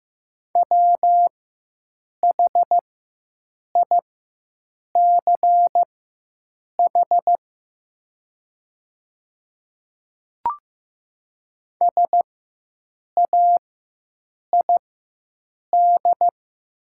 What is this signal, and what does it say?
Morse code, heard as a keyed tone, about 15 wpm, spaced as for 7 wpm, 700 Hz. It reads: WHICH SAID
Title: 4x Character Spacing - Top 100 Words - 15wpm